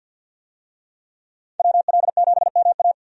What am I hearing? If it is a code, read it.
Morse code, heard as a keyed tone, about 52 wpm, 700 Hz. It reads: WL7MA
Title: USA Call Signs - 5 Characters - 50wpm